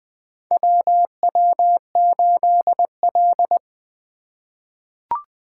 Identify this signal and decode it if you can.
Morse code, heard as a keyed tone, about 20 wpm, 700 Hz. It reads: WW8L
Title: 2023 POTA Activator Call Signs - 20wpm